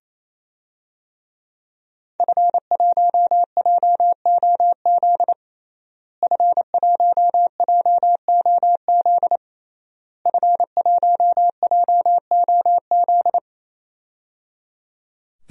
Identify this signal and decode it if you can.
Morse code, heard as a keyed tone, about 27 wpm, 700 Hz. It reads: F1JOZ F1JOZ F1JOZ
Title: DXCC-100 Call Signs - 5 Characters - Repeated 3x - 28wpm